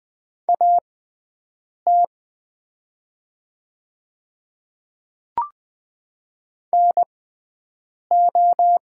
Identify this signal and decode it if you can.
Morse code, heard as a keyed tone, about 20 wpm, spaced as for 7 wpm, 700 Hz. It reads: AT NO
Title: 6x Character Spacing - Top 100 Words - 20wpm